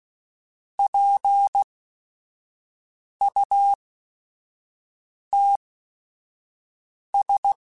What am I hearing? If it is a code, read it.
Morse code, heard as a keyed tone, about 16 wpm, spaced as for 5 wpm, 800 Hz. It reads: PUTS
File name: Technical Subjects / Technical Subjects-mp3s / OR Gate @5